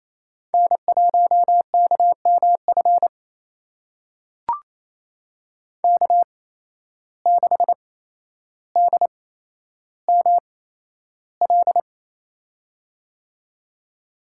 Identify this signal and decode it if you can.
Morse code, heard as a keyed tone, about 28 wpm, spaced as for 12 wpm, 700 Hz. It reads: N1KMF K6DML
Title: USA Call Signs - 8x Character Spacing - 28wpm